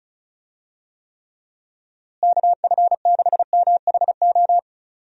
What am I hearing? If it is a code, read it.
Morse code, heard as a keyed tone, about 36 wpm, 700 Hz. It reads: KF6MHO
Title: USA Call Signs - 6 Characters - Repeated 2x - 35wpm